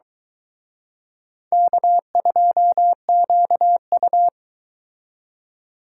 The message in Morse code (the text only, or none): K2QU